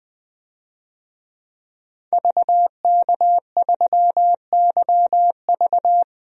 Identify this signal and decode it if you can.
Morse code, heard as a keyed tone, about 21 wpm, 700 Hz. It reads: VK3YV
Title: DXCC-100 Call Signs - 5 Characters - 20wpm